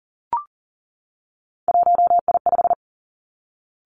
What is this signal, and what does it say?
Morse code, heard as a keyed tone, about 38 wpm, 700 Hz. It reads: 1I5
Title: Three Challenging Letter-Number Combinations 40wpm